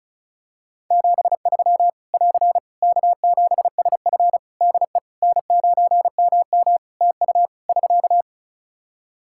Morse code, heard as a keyed tone, about 34 wpm, 700 Hz. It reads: 73 <AR> K7SF DE N9MM TU <SK>
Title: K1USN SST QSO Practice 35wpm